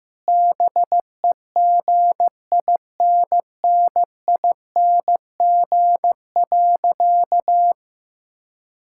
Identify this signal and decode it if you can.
Morse code, heard as a keyed tone, about 16 wpm, 700 Hz. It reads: BEGINNING.
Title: Sentences from Top 1000 Words - Review Part 5 - 15wpm